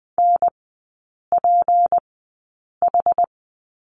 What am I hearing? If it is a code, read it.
Morse code, heard as a keyed tone, about 20 wpm, spaced as for 8 wpm, 700 Hz. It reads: NPH
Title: 152 Lesson 25e - Letter p - Three Characters 20wpm